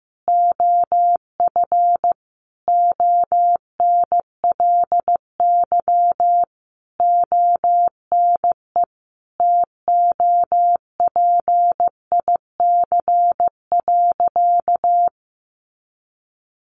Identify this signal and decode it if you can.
Morse code, heard as a keyed tone, about 15 wpm, 700 Hz. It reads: OF ONLY ONE TOPIC.
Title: Sentences from Top 1200 Words 15wpm